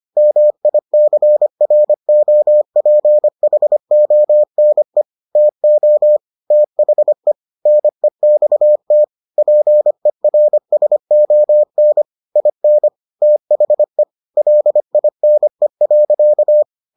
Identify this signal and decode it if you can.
Morse code, heard as a keyed tone, about 25 wpm, 600 Hz. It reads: MICROPHONE TO THE NEXT PERSON IN THE LINE.